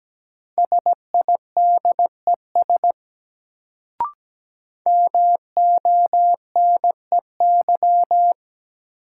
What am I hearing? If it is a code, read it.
Morse code, heard as a keyed tone, about 17 wpm, 700 Hz. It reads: SIDES MONEY
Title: Sets of 4 Words - New Word Drill - 17wpm